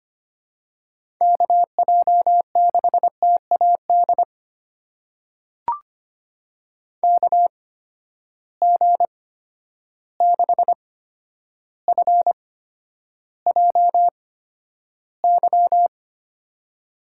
Morse code, heard as a keyed tone, about 25 wpm, spaced as for 11 wpm, 700 Hz. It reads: KJ6TAD KG6FJY
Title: USA Call Signs - 6 Characters - 8x Character Spacing - 25wpm